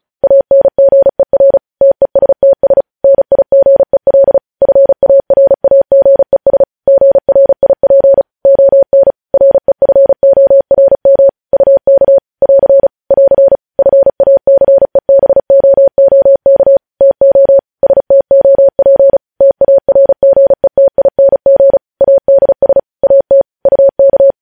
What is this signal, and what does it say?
Morse code, heard as a keyed tone, about 36 wpm, 550 Hz. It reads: ANGER TESTS NIGEL FARAGES GRIP ON REFORM UK <AR> <AR> FACEBOOK TO STOP TARGETING ADS AT UK
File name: podcast / News headlines-1x-35x35